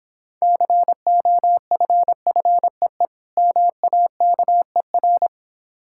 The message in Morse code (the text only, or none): COFFEE MAKER